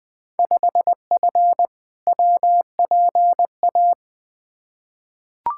Morse code, heard as a keyed tone, about 20 wpm, 700 Hz. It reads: 5F WPA
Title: Field Day Exchanges Encore - 20wpm